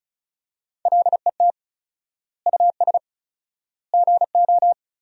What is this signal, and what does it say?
Morse code, heard as a keyed tone, about 36 wpm, 700 Hz. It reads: LET US GO
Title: 4x Word-Spacing - Sentences from Top 300 Words - 35wpm